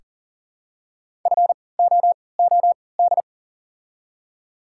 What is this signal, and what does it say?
Morse code, heard as a keyed tone, about 39 wpm, spaced as for 22 wpm, 700 Hz. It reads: FOOD
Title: 3x Character Spacing - Top 300 Words - 40wpm